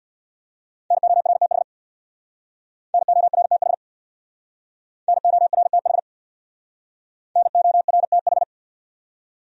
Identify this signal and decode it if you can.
Morse code, heard as a keyed tone, about 66 wpm, 700 Hz. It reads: NORTH NORTH NORTH NORTH